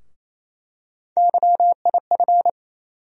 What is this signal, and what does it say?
Morse code, heard as a keyed tone, about 28 wpm, 700 Hz. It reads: YIF
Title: USA Call Sign Postfixes - 3 Letters 28wpm